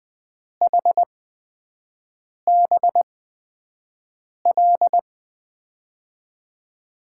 Morse code, heard as a keyed tone, about 20 wpm, spaced as for 5 wpm, 700 Hz. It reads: HBL
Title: USA Call Signs - 6 Characters - 8x Character Spacing - 20wpm